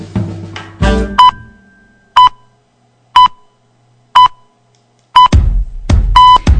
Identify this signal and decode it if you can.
Radio time signal (time pips).